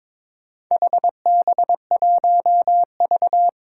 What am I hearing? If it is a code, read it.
Morse code, heard as a keyed tone, about 22 wpm, 700 Hz. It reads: HB1V
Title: Four Challenging Letter-Number Combinations 22wpm